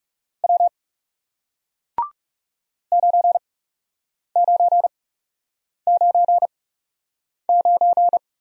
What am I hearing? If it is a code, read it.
Morse code, heard as a keyed tone, about 38 wpm, spaced as for 8 wpm, 700 Hz. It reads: W 9999